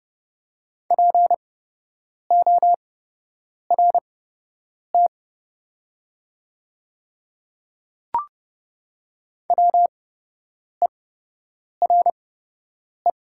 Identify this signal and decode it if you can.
Morse code, heard as a keyed tone, about 30 wpm, spaced as for 8 wpm, 700 Hz. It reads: PORT WERE
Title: 8x Character Spacing - Top 200 Words - 30wpm